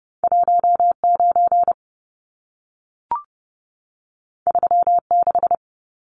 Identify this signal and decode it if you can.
Morse code, heard as a keyed tone, about 30 wpm, 700 Hz. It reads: 19 36
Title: Numbers - 2-Digit 30wpm